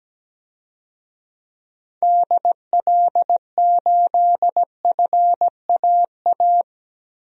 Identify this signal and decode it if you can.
Morse code, heard as a keyed tone, about 17 wpm, 700 Hz. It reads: DL8FAA
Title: DXCC-100 Call Signs - 6 Characters - Repeated 2x - 17wpm